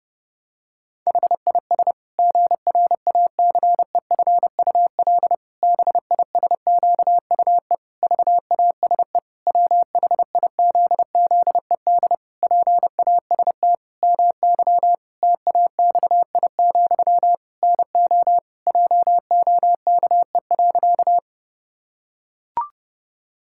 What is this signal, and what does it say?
Morse code, heard as a keyed tone, about 30 wpm, 700 Hz. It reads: HIS GRACEFUL BISQUE VASE WHIZZED PAST MY TAXI, NO JOKE.
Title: Pangram Challenge v2 - 30wpm